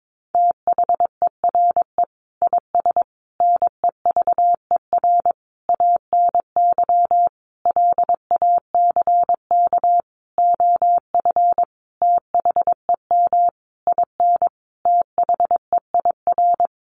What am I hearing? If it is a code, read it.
Morse code, heard as a keyed tone, about 22 wpm, 700 Hz. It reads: THERE IS NEVER ANY LACK OF THEM IN THEIR
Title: Sentences from Top 1100 Words 22wpm